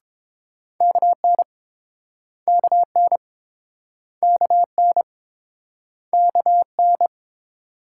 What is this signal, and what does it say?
Morse code, heard as a keyed tone, about 28 wpm, 700 Hz. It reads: KN KN KN KN